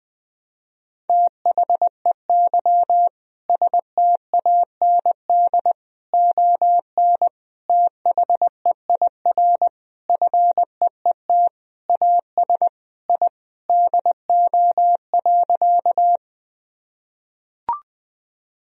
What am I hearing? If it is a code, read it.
Morse code, heard as a keyed tone, about 20 wpm, 700 Hz. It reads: THEY STAND ON THEIR FEET AS I DO.
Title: Sentences from Top 1000 Words - Review Part 1 - 20wpm